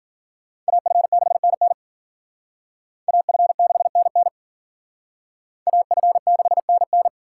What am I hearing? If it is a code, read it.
Morse code, heard as a keyed tone, about 50 wpm, 700 Hz. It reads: AF6NN AF6NN AF6NN
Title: Speed Racing - USA Call Signs 40wpm